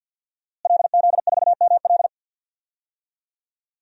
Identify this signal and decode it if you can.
Morse code, heard as a keyed tone, about 53 wpm, 700 Hz. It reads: LZ4ML
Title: DXCC-100 Call Signs - 5 Characters - Repeated 2x - 50wpm